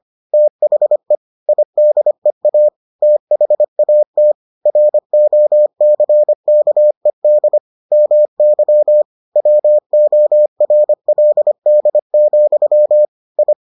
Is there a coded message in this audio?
Yes: Morse code from 0.3 to 13.5 s, heard as a keyed tone, about 25 wpm, 600 Hz. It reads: THE IDEA THAT ROCKED MY WORLD, I